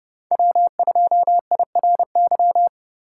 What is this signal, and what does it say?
Morse code, heard as a keyed tone, about 30 wpm, 700 Hz. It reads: W2IRY